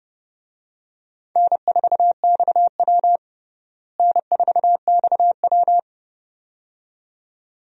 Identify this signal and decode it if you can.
Morse code, heard as a keyed tone, about 32 wpm, 700 Hz. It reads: N4XW N4XW
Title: USA Call Signs - 4 Characters - Repeated 2x - 30wpm